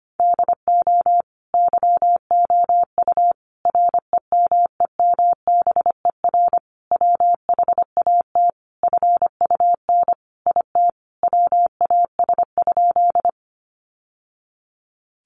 Morse code, heard as a keyed tone, about 25 wpm, 700 Hz. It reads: DO YOU REMEMBER WHAT FUN IT WAS?
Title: Sentences from Top 700 Words 25wpm